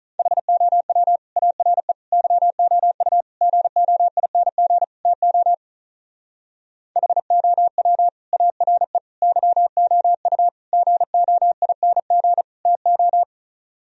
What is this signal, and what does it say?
Morse code, heard as a keyed tone, about 43 wpm, 700 Hz. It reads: HOW ARE YOU GOING TO HOW ARE YOU GOING TO
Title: Speed-Racing - Sets of 5 Words 35wpm